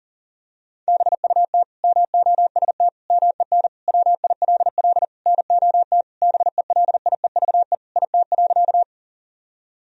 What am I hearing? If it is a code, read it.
Morse code, heard as a keyed tone, about 40 wpm, 700 Hz. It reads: BUT MOST MEN WILL NOT BELIEVE IT.
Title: Sentences from Top 1000 Words - Review Part 1 - 40wpm